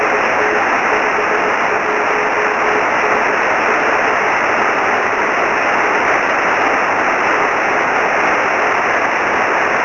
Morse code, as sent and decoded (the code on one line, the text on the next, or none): none